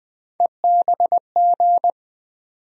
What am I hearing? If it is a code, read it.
Morse code, heard as a keyed tone, about 20 wpm, 700 Hz. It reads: EBG
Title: USA Call Sign Postfixes - 3 Letters 20wpm